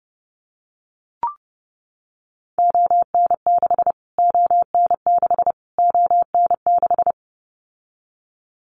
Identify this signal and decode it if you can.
Morse code, heard as a keyed tone, about 30 wpm, 700 Hz. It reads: ON6 ON6 ON6
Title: Callsign Prefixes for Top 25 DXCC - Repeated 3x - 30wpm